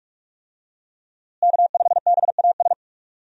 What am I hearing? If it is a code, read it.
Morse code, heard as a keyed tone, about 42 wpm, 700 Hz. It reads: K5BAS
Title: USA Call Signs - 5 Characters - Repeated 3x - 45wpm